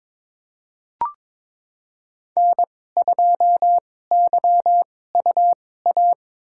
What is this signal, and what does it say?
Morse code, heard as a keyed tone, about 21 wpm, spaced as for 16 wpm, 700 Hz. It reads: N2YUA
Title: USA Call Signs - 5 Characters - 2x Character Spacing - 22wpm